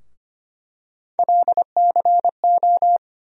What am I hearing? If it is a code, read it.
Morse code, heard as a keyed tone, about 25 wpm, 700 Hz. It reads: LCO
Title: USA Call Sign Postfixes - 3 Letters 25wpm